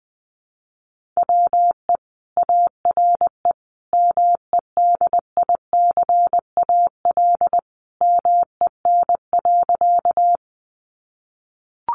Morse code, heard as a keyed tone, about 20 wpm, 700 Hz. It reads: WE ARE MEDICAL MEN.
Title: Sentences from Top 800 Words 20wpm